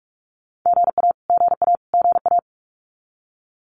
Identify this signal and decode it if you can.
Morse code, heard as a keyed tone, about 44 wpm, 700 Hz. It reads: GA GA GA
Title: Abbreviations Repeated 3x - 45wpm